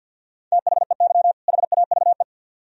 Morse code, heard as a keyed tone, about 64 wpm, 700 Hz. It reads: THEY HAVE